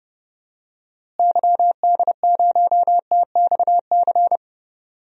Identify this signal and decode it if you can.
Morse code, heard as a keyed tone, about 30 wpm, 700 Hz. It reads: YD0TXC
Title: DXCC-100 Call Signs - 6 Characters - Repeated 3x - 30wpm